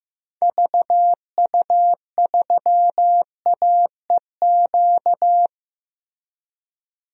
Morse code, heard as a keyed tone, about 15 wpm, 700 Hz. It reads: VU3AEQ
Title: DXCC-100 Call Signs - 6 Characters - Repeated 3x - 15wpm